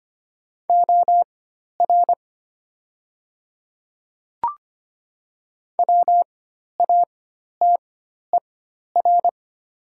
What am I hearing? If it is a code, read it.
Morse code, heard as a keyed tone, about 25 wpm, spaced as for 12 wpm, 700 Hz. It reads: OR WATER
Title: 4x Character Spacing - Top 100 Words - 25wpm